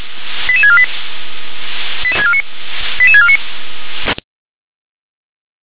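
Five-tone selective call (ZVEI), heard as quick runs of five tones, tones of about 70 ms. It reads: 88648 88649
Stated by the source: ZVEI five-tone selective call 88675 received with an SDR.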